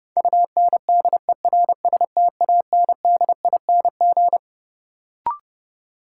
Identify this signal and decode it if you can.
Morse code, heard as a keyed tone, about 31 wpm, 700 Hz. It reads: UNDERSTANDING
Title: Sentences from Top 1000 Words - Review Part 3 - New Word Drill - 30wpm